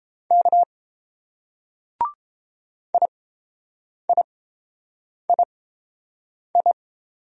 Morse code, heard as a keyed tone, about 27 wpm, spaced as for 7 wpm, 700 Hz. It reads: K IIII
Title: Speed-Racing Single Letters 22wpm